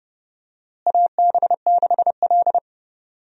Morse code, heard as a keyed tone, about 32 wpm, 700 Hz. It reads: AB6L
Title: USA Call Signs - 4 Characters - Repeated 3x - 30wpm